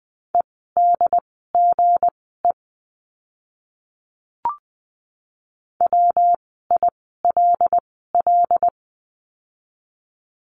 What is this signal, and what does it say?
Morse code, heard as a keyed tone, about 20 wpm, spaced as for 15 wpm, 700 Hz. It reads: EDGE WILL
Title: 160 Lesson 26f - Letter g - Words 20wpm